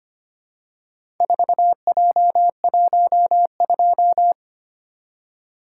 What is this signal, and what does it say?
Morse code, heard as a keyed tone, about 26 wpm, 700 Hz. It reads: 4J12
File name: practice / 25wpm / Four Letter-Number Combinations - 25wpm